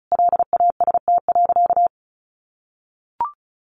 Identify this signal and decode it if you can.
Morse code, heard as a keyed tone, about 34 wpm, 700 Hz. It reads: LAST.
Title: Sentences from Top 1400 Words 35wpm